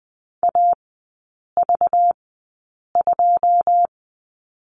Morse code, heard as a keyed tone, about 20 wpm, spaced as for 8 wpm, 700 Hz. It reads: AV2
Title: 191 Lesson 31e - Letter v - Three Characters 20wpm